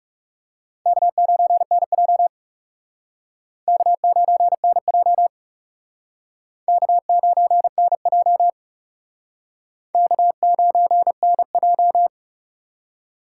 Morse code, heard as a keyed tone, about 37 wpm, 700 Hz. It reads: K9NJ K9NJ K9NJ K9NJ
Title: Speed Racing - USA 1x2 Call Signs 30wpm